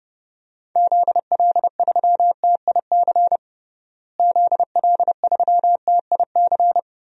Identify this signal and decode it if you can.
Morse code, heard as a keyed tone, about 30 wpm, 700 Hz. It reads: ZL3TIC ZL3TIC
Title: DXCC-100 Call Signs - 6 Characters - Repeated 2x - 30wpm